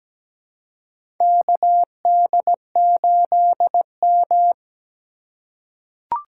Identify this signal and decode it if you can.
Morse code, heard as a keyed tone, about 17 wpm, 700 Hz. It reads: KD8M
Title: USA Call Signs - 4 Characters - 4x Character Spacing - 17wpm